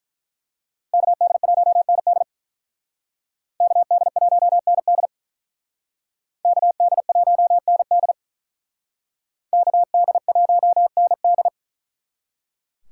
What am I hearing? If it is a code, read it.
Morse code, heard as a keyed tone, about 42 wpm, 700 Hz. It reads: KD1ND KD1ND KD1ND KD1ND
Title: Speed Racing - USA 2x2 Call Signs 35wpm